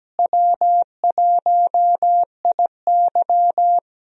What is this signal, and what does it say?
Morse code, heard as a keyed tone, about 17 wpm, 700 Hz. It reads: W1IY
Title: USA Call Signs - 4 Characters - Repeated 2x - 17wpm